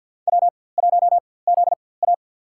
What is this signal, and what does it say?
Morse code, heard as a keyed tone, about 49 wpm, spaced as for 23 wpm, 700 Hz. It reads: W1ZA
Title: USA Call Signs - 4x Character Spacing - 50wpm